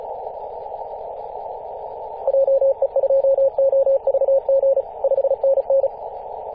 Morse code, heard as a keyed tone, about 34 wpm, 540 Hz. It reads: JE2OVG 5NN
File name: 2016/Apr/09 0838z HV0A 18.071MHz CW